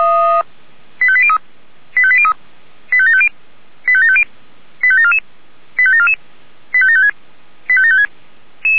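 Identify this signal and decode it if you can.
Five-tone selective call (ZVEI), heard as quick runs of five tones, tones of about 70 ms. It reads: 85803 85803 85750 85750 85744 85744 85756 85756
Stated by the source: ZVEI five-tone selective call received on an SDR.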